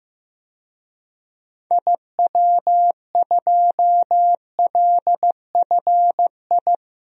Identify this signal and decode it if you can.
Morse code, heard as a keyed tone, about 15 wpm, 700 Hz. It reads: IW2LFI
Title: DXCC-100 Call Signs - 6 Characters - 15wpm